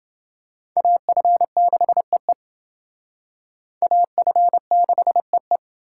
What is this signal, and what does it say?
Morse code, heard as a keyed tone, about 28 wpm, 700 Hz. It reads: AF6EE AF6EE